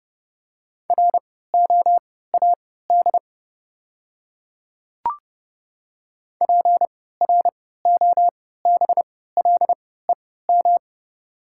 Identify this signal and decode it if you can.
Morse code, heard as a keyed tone, about 30 wpm, spaced as for 17 wpm, 700 Hz. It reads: ROAD PROBLEM
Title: 3x Character Spacing - Top 500 Words - 30wpm